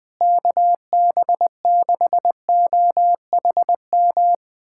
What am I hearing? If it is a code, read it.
Morse code, heard as a keyed tone, about 20 wpm, 700 Hz. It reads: KB6OHM
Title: USA Callsign DE Callsign 20wpm